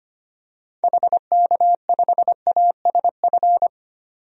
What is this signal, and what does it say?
Morse code, heard as a keyed tone, about 25 wpm, 700 Hz. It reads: HK5ASF
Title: DXCC-100 Call Signs - 6 Characters - 25wpm